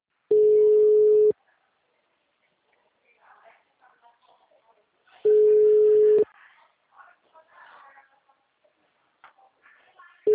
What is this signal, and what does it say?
Ringback tone (Europe, 425 Hz), with speech.